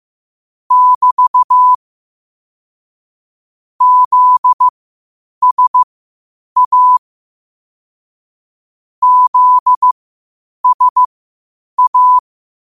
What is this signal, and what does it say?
Morse code, heard as a keyed tone, about 15 wpm, spaced as for 8 wpm, 1000 Hz. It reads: <BT> ZSA ZSA
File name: qotdfiles / 8wpm